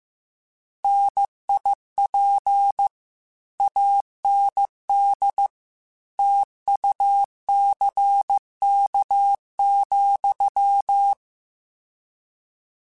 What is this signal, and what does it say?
Morse code, heard as a keyed tone, about 15 wpm, 800 Hz. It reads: NIP AND TUCK,
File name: Misc / Misc-mp3s / Word Pairs @13